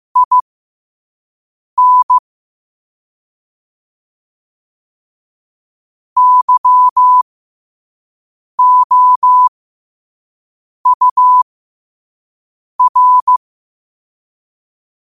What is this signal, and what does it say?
Morse code, heard as a keyed tone, about 15 wpm, spaced as for 5 wpm, 1000 Hz. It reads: IN YOUR
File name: qotdfiles / 5wpm